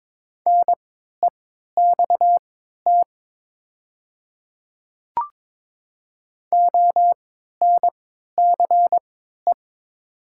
Morse code, heard as a keyed tone, about 22 wpm, spaced as for 13 wpm, 700 Hz. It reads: NEXT ONCE
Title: 3x Character Spacing - Top 500 Words - 22wpm